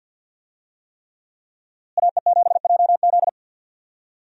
Morse code, heard as a keyed tone, about 46 wpm, 700 Hz. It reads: AE7JZ
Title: USA Call Signs - 5 Characters - Repeated 3x - 50wpm